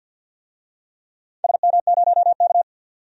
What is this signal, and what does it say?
Morse code, heard as a keyed tone, about 46 wpm, 700 Hz. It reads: SM0K
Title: DXCC-100 Call Signs - 4 Characters - 50wpm